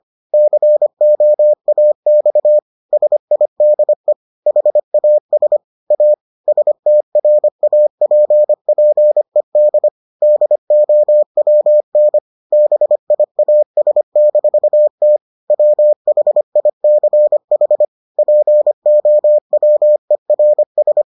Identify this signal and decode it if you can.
Morse code, heard as a keyed tone, about 25 wpm, 600 Hz. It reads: COAX SIDE HAS A STRAPPED DOWN BIAS-T WHICH POWERS